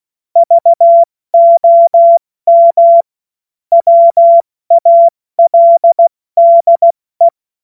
Morse code, heard as a keyed tone, about 16 wpm, 680 Hz. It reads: VOM WALDE
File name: Morsekurs / Witze / Witz 14. März 2023 70 F80